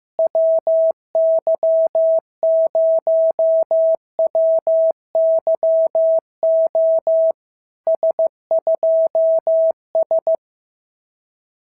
Morse code, heard as a keyed tone, about 14 wpm, 650 Hz. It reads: WY0WYO S2S
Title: SOTA QSO Practice 15wpm